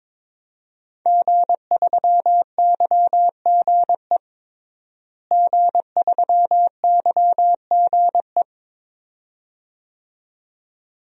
Morse code, heard as a keyed tone, about 22 wpm, 700 Hz. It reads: G3YGE G3YGE
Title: DXCC-100 Call Signs - 5 Characters - Repeated 2x - 22wpm